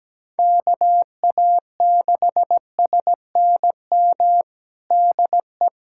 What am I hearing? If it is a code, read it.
Morse code, heard as a keyed tone, about 17 wpm, 700 Hz. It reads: KA6SNM DE
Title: USA Callsign DE Callsign - 6 Characters - 17wpm